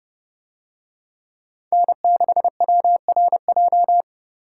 Morse code, heard as a keyed tone, about 31 wpm, 700 Hz. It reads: N6WRJ